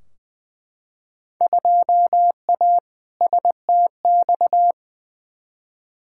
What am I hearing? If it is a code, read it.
Morse code, heard as a keyed tone, about 20 wpm, 700 Hz. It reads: 2A STX
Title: Field Day Exchanges - 20wpm